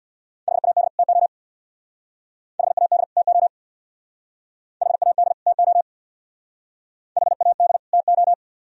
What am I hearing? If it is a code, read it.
Morse code, heard as a keyed tone, about 60 wpm, 700 Hz. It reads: HAD TO HAD TO HAD TO HAD TO
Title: Speed-Racing - Sets of 2 Words 50wpm